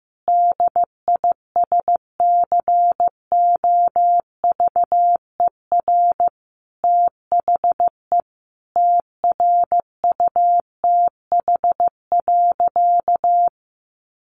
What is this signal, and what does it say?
Morse code, heard as a keyed tone, about 15 wpm, 700 Hz. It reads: DISCOVER THE TRUTH.
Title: Sentences from Top 1100 Words 15wpm